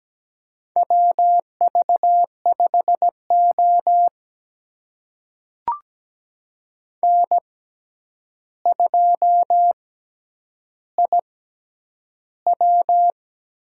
Morse code, heard as a keyed tone, about 17 wpm, spaced as for 9 wpm, 700 Hz. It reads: WV5O N2IW
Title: USA Call Signs - 4 Characters - 6x Character Spacing - 17wpm